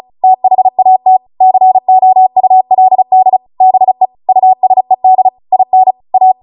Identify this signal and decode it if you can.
Morse code, heard as a keyed tone, about 36 wpm, 750 Hz. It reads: THAT COULD BE USED IN A
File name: Morse / 35WPM